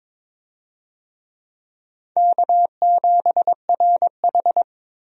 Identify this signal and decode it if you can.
Morse code, heard as a keyed tone, about 22 wpm, 700 Hz. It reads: K7RH
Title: USA Call Signs - 4 Characters - 22wpm